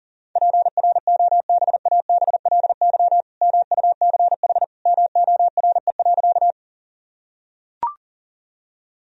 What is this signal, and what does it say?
Morse code, heard as a keyed tone, about 40 wpm, 700 Hz. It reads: PROBABLY MUCH MORE.
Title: Sentences from Top 1000 Words - Review Part 4 - 40wpm